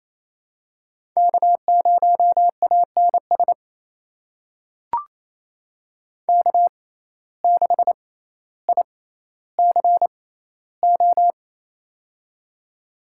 Morse code, heard as a keyed tone, about 28 wpm, spaced as for 14 wpm, 700 Hz. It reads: K0ANS K6ICO
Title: USA Call Signs - 5 Characters - 6x Character Spacing - 28wpm